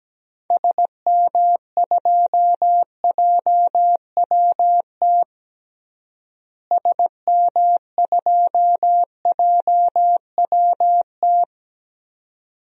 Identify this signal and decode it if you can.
Morse code, heard as a keyed tone, about 17 wpm, 700 Hz. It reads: SM2JWT SM2JWT
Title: DXCC-100 Call Signs - 6 Characters - Repeated 2x - 17wpm